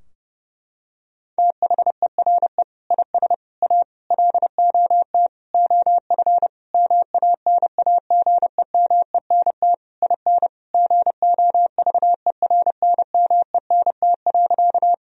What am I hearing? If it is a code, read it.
Morse code, heard as a keyed tone, about 30 wpm, 700 Hz. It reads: THERE IS A LOT OF MANAGEMENT IN GOVERNMENT.
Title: Sentences from Top 1000 Words - Review Part 4 - 30wpm